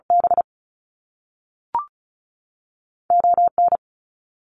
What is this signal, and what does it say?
Morse code, heard as a keyed tone, about 36 wpm, 700 Hz. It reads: B O N
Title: Canadian Provinces and Territories 35wpm